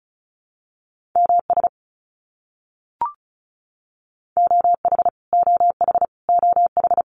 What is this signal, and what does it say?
Morse code, heard as a keyed tone, about 34 wpm, 700 Hz. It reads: MS OH OH OH
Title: US States Repeated 3x 35wpm